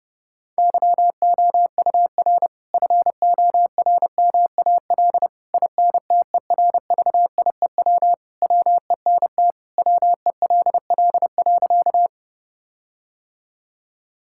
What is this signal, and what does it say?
Morse code, heard as a keyed tone, about 30 wpm, 700 Hz. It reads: YOUR FORMAL INTERVIEW WENT WELL.
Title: Sentences from Top 1600 Words 30wpm